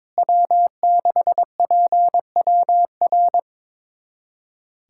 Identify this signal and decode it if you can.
Morse code, heard as a keyed tone, about 22 wpm, 700 Hz. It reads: W6PWR
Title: USA Callsign DE Callsign - 5 Characters - 22wpm